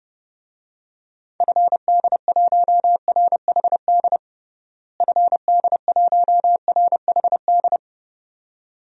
Morse code, heard as a keyed tone, about 31 wpm, 700 Hz. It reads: FD1RHD FD1RHD
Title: DXCC-100 Call Signs - 6 Characters - Repeated 2x - 30wpm